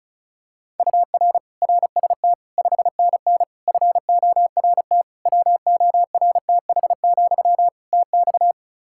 Morse code, heard as a keyed tone, about 34 wpm, 700 Hz. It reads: UR RST 5NN FORT WORTH, TX
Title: Practice QSO - Exchange 1 - Beginning - 35wpm